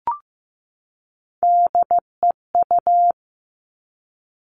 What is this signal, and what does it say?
Morse code, heard as a keyed tone, about 15 wpm, 700 Hz. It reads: DEU
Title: Three Challenging Letter-Number Combinations 15wpm